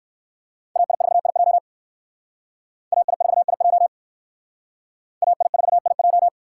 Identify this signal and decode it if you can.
Morse code, heard as a keyed tone, about 68 wpm, 700 Hz. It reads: AI4IJ AI4IJ AI4IJ